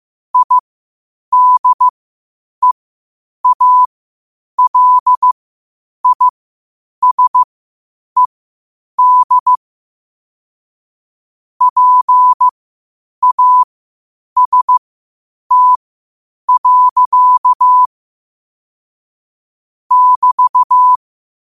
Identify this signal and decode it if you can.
Morse code, heard as a keyed tone, about 15 wpm, spaced as for 8 wpm, 1000 Hz. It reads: IDEALISED PAST. <BT>
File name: qotdfiles / 8wpm